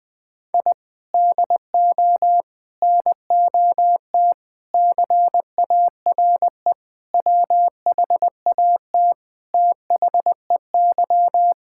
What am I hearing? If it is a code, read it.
Morse code, heard as a keyed tone, about 20 wpm, 700 Hz. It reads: I DO NOT CARE WHAT THEY
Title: Sentences with 5-Word Phrases - 20wpm